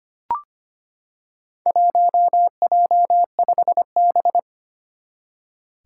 Morse code, heard as a keyed tone, about 25 wpm, 700 Hz. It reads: 1J5B
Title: Four Challenging Letter-Number Combinations 25wpm